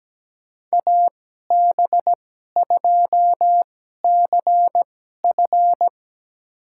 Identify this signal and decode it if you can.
Morse code, heard as a keyed tone, about 17 wpm, spaced as for 12 wpm, 700 Hz. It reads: AB2CF
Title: USA Call Signs - 5 Characters - 2x Character Spacing - 17wpm